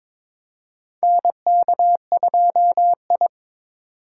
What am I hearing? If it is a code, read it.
Morse code, heard as a keyed tone, about 22 wpm, 700 Hz. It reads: NK2I
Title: USA Call Signs - 2x Character Spacing - 22wpm